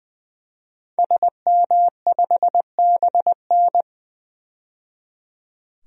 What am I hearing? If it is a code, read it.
Morse code, heard as a keyed tone, about 21 wpm, 700 Hz. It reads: SM5BN
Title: DXCC-100 Call Signs - 5 Characters - 20wpm